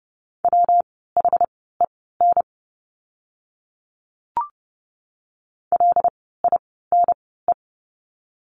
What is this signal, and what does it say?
Morse code, heard as a keyed tone, about 30 wpm, spaced as for 17 wpm, 700 Hz. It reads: WHEN LINE
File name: course / 30:10wpm / 115 Lesson 19f - Letter w - Words 30wpm